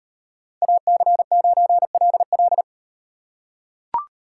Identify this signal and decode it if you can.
Morse code, heard as a keyed tone, about 38 wpm, 700 Hz. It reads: AC9LL